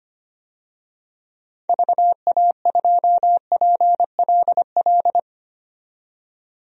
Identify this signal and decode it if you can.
Morse code, heard as a keyed tone, about 25 wpm, 700 Hz. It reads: VA2PLL
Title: DXCC-100 Call Signs - 6 Characters - 25wpm